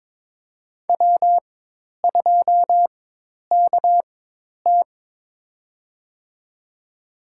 Morse code, heard as a keyed tone, about 22 wpm, spaced as for 10 wpm, 700 Hz. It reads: W2KT
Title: USA Call Signs - 4 Characters - 4x Character Spacing - 22wpm